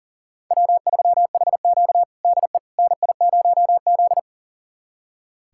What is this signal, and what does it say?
Morse code, heard as a keyed tone, about 40 wpm, 700 Hz. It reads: W3HQ DE NI0Z
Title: USA Callsign DE Callsign - 4 Characters - 40wpm